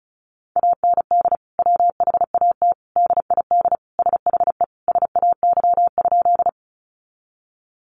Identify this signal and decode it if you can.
Morse code, heard as a keyed tone, about 34 wpm, 700 Hz. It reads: AND WHAT DID SHE SAY?
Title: Sentences from Top 200 Words 35wpm